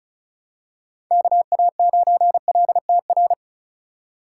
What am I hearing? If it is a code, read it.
Morse code, heard as a keyed tone, about 34 wpm, 700 Hz. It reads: KA9LTR
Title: USA Call Signs - 2x Character Spacing - 35wpm